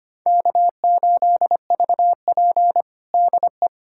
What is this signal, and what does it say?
Morse code, heard as a keyed tone, about 25 wpm, 700 Hz. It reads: K8VP DE
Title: USA Callsign DE Callsign - 4 Characters - 25wpm